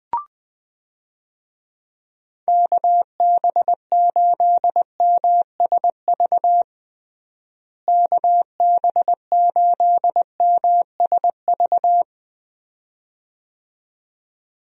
Morse code, heard as a keyed tone, about 20 wpm, 700 Hz. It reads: KB8MSV KB8MSV
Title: USA Call Signs - 6 Characters - Repeated 2x - 20wpm